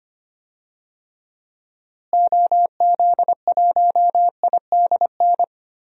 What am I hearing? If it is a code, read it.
Morse code, heard as a keyed tone, about 25 wpm, 700 Hz. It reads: OZ1IDN
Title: DXCC-100 Call Signs - 6 Characters - Repeated 3x - 25wpm